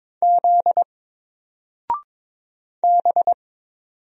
Morse code, heard as a keyed tone, about 22 wpm, 700 Hz. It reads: Z B
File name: Single-Letter-Number-22wpm